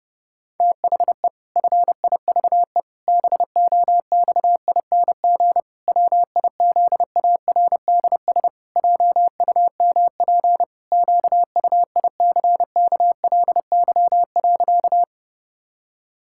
Morse code, heard as a keyed tone, about 32 wpm, 700 Hz. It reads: THE FIVE BOXING WIZARDS JUMP QUICKLY.
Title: Pangram Challenge v2 - 30wpm